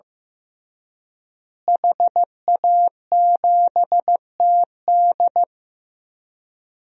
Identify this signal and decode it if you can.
Morse code, heard as a keyed tone, about 15 wpm, 700 Hz. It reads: HA7TD
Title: DXCC-100 Call Signs - 5 Characters - Repeated 3x - 15wpm